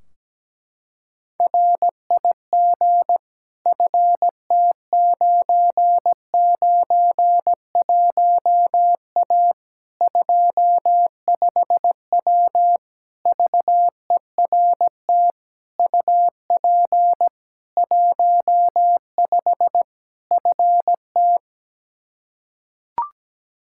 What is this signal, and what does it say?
Morse code, heard as a keyed tone, about 17 wpm, 700 Hz. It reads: RIG FT991A 25W VERT UP 15 FT
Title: Practice QSO - Exchange 2 - Info - 17wpm